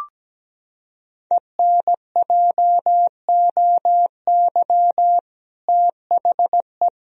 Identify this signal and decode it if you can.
Morse code, heard as a keyed tone, about 17 wpm, 700 Hz. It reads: ENJOY THE
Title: Sentences from Top 1000 Words - Review Part 4 - 17wpm